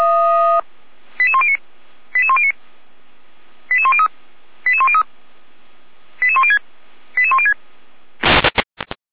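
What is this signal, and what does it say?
Five-tone selective call (ZVEI), heard as quick runs of five tones, tones of about 70 ms. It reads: 88109 88109 88103 88103 88106 88106
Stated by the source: ZVEI five-tone call 88102 received on SDR.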